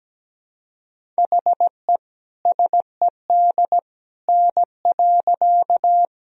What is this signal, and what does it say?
Morse code, heard as a keyed tone, about 17 wpm, 700 Hz. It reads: HE SED N.
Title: Sentences from Top 100 Words and Abbreviations 17wpm